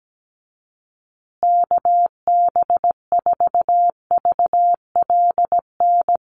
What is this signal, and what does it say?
Morse code, heard as a keyed tone, about 17 wpm, 700 Hz. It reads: KB4VLN
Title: USA Call Signs Repeated 2x 17wpm